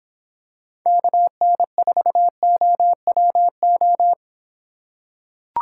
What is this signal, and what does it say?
Morse code, heard as a keyed tone, about 26 wpm, 700 Hz. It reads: KN4OWO